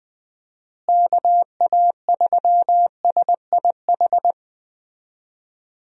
Morse code, heard as a keyed tone, about 20 wpm, 700 Hz. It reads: KA3SIH
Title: USA Call Signs - 6 Characters - Repeated 3x - 20wpm